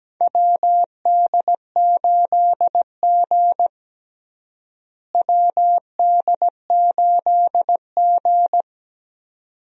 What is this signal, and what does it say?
Morse code, heard as a keyed tone, about 17 wpm, 700 Hz. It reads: WD8G WD8G
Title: USA Call Signs - 4 Characters - Repeated 2x - 17wpm